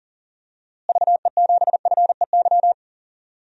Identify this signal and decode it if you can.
Morse code, heard as a keyed tone, about 40 wpm, 700 Hz. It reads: VE7FEY
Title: DXCC-100 Call Signs - 6 Characters - Repeated 3x - 40wpm